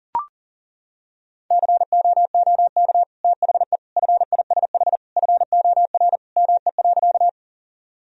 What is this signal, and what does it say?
Morse code, heard as a keyed tone, about 40 wpm, 700 Hz. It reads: COOK THE FISH FOR ME.
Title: Sentences from Top 1000 Words - Review Part 4 - 40wpm